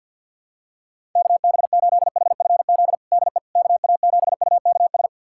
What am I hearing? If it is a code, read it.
Morse code, heard as a keyed tone, about 52 wpm, 700 Hz. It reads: KB8HFZ DE KA7UKS
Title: USA Callsign DE Callsign - 6 Characters - 50wpm